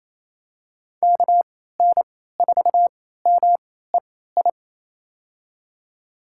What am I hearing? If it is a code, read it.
Morse code, heard as a keyed tone, about 28 wpm, spaced as for 16 wpm, 700 Hz. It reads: KN4MEI
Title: USA Call Signs - 6 Characters - 3x Character Spacing - 28wpm